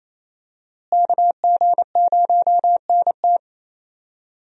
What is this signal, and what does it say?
Morse code, heard as a keyed tone, about 28 wpm, 700 Hz. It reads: KG0NT
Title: USA Call Signs - 5 Characters - Repeated 2x - 28wpm